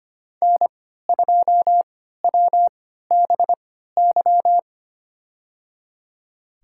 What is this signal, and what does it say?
Morse code, heard as a keyed tone, about 25 wpm, spaced as for 14 wpm, 700 Hz. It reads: N2WBY
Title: USA Call Signs - 5 Characters - 3x Character Spacing - 25wpm